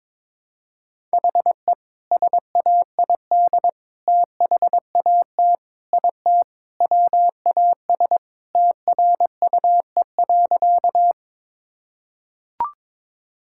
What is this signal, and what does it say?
Morse code, heard as a keyed tone, about 22 wpm, 700 Hz. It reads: HE SAID THAT IT WAS TRUE.
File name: practice / 22wpm / Sentences with 2-Word Phrases - 22wpm